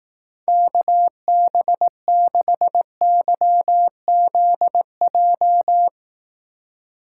Morse code, heard as a keyed tone, about 18 wpm, 700 Hz. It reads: KB6YZJ
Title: Speed Racing - USA 2x3 Call Signs 15wpm